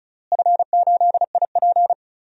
Morse code, heard as a keyed tone, about 36 wpm, 690 Hz. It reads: F8IP
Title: DXCC-100 Call Signs - 4 Characters - Repeated 2x - 35wpm